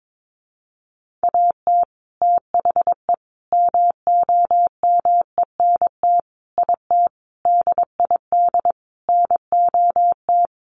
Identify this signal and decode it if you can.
Morse code, heard as a keyed tone, about 22 wpm, 700 Hz. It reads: AT THE MOMENT IT DID NOT
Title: Sentences from Top 1200 Words 22wpm